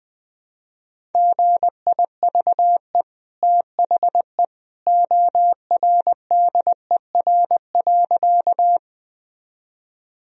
Morse code, heard as a keyed tone, about 21 wpm, 700 Hz. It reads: GIVE THE ORDER.
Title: Sentences from Top 1000 Words - Review Part 2 - 20wpm